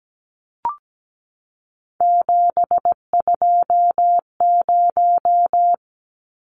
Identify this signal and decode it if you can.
Morse code, heard as a keyed tone, about 17 wpm, 700 Hz. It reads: 720
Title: Numbers - 3-Digit 17wpm